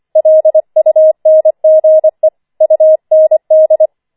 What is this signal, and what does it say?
Morse code, heard as a keyed tone, about 25 wpm, 630 Hz. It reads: LUNGE UND
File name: 25wpm